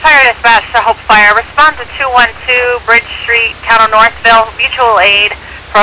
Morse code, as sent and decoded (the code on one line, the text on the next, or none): none